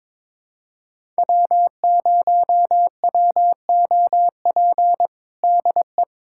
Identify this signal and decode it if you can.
Morse code, heard as a keyed tone, about 21 wpm, 700 Hz. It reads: W0WOP DE
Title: USA Callsign DE Callsign - 5 Characters - 22wpm